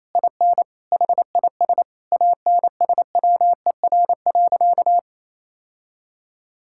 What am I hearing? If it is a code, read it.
Morse code, heard as a keyed tone, about 28 wpm, 700 Hz. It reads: IN HIS ANSWER.
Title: Sentences from Top 1600 Words 28wpm